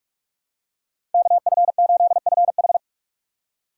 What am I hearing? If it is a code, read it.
Morse code, heard as a keyed tone, about 48 wpm, 700 Hz. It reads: KF8FH